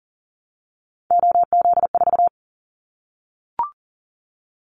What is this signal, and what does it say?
Morse code, heard as a keyed tone, about 41 wpm, 700 Hz. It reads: OZ4
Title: Callsign Prefixes for Top 100 DXCC - 40wpm